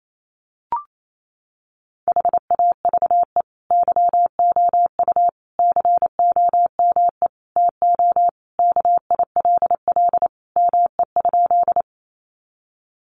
Morse code, heard as a keyed tone, about 28 wpm, 700 Hz. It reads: HAVE YOU COME TO KILL ME?
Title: Sentences from Top 1400 Words 28wpm